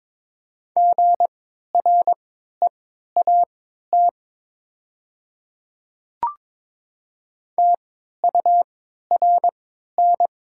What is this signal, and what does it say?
Morse code, heard as a keyed tone, about 22 wpm, spaced as for 13 wpm, 700 Hz. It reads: GREAT TURN
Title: 3x Character Spacing - Top 300 Words - 22wpm